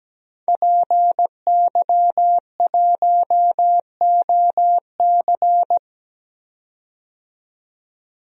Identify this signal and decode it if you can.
Morse code, heard as a keyed tone, about 17 wpm, 700 Hz. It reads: PY1OC
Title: DXCC-100 Call Signs - 5 Characters - 17wpm